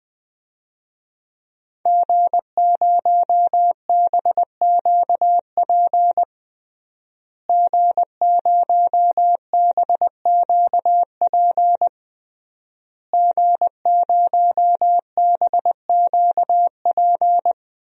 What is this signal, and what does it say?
Morse code, heard as a keyed tone, about 20 wpm, 700 Hz. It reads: G0BQP G0BQP G0BQP
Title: DXCC-100 Call Signs - 5 Characters - Repeated 3x - 20wpm